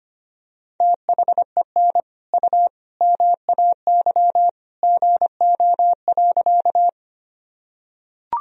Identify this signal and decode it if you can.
Morse code, heard as a keyed tone, about 25 wpm, 700 Hz. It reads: THEN U MAY GO.
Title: Sentences from Top 100 Words and Abbreviations 25wpm